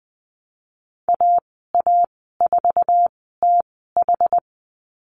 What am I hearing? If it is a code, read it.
Morse code, heard as a keyed tone, about 20 wpm, spaced as for 14 wpm, 700 Hz. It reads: AA4TH
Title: 062 Lesson 11g - Letter h - Call Signs 20wpm